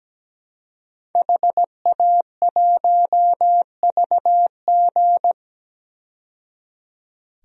Morse code, heard as a keyed tone, about 17 wpm, 700 Hz. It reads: HA1VG
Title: DXCC-100 Call Signs - 5 Characters - 17wpm